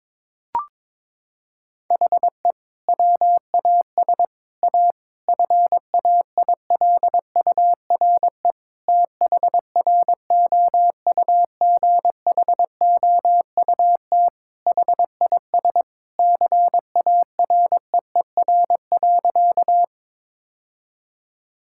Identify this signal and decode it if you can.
Morse code, heard as a keyed tone, about 22 wpm, 700 Hz. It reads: HE WAS A FAILURE THROUGHOUT HIS CAREER.
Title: Sentences from Top 1000 Words - Review Part 6 - 22wpm